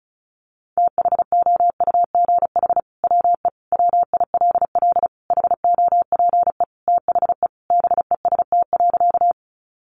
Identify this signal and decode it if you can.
Morse code, heard as a keyed tone, about 34 wpm, 700 Hz. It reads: THOUGH WE WILL HOPE THE BEST.
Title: Sentences from Top 1000 Words 35wpm